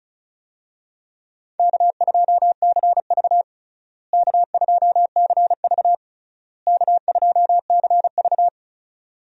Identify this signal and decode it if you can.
Morse code, heard as a keyed tone, about 36 wpm, 700 Hz. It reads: K2CV K2CV K2CV
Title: USA Call Signs - 4 Characters - Repeated 3x - 35wpm